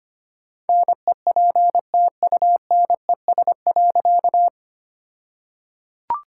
Morse code, heard as a keyed tone, about 25 wpm, 700 Hz. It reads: NEPTUNES.
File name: practice / 25wpm / All-Age Appropriate Jokes - 25wpm